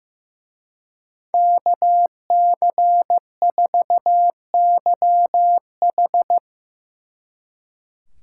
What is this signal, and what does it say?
Morse code, heard as a keyed tone, about 16 wpm, 700 Hz. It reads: KC4YH
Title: Speed Racing - USA 2x2 Call Signs 15wpm